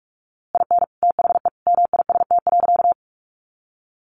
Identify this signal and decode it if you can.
Morse code, heard as a keyed tone, about 44 wpm, 700 Hz. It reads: IN THE MIST.
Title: A Sound Of Thunder 45wpm